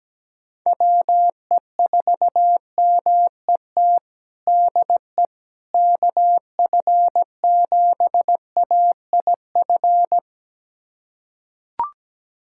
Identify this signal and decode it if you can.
Morse code, heard as a keyed tone, about 17 wpm, 700 Hz. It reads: WE4MET DE KF7AIF
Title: USA Callsign DE Callsign - 6 Characters - 17wpm